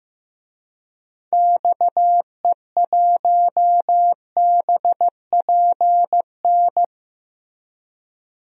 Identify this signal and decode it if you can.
Morse code, heard as a keyed tone, about 15 wpm, 700 Hz. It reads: XE1BPN
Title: DXCC-100 Call Signs - 6 Characters - Repeated 2x - 15wpm